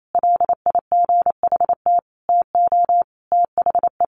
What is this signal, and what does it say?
Morse code, heard as a keyed tone, about 28 wpm, 700 Hz. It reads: LIGHT TO THE
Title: Sentences from Top 1500 Words 28wpm